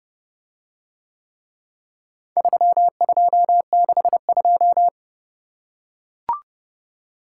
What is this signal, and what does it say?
Morse code, heard as a keyed tone, about 31 wpm, 700 Hz. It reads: 3262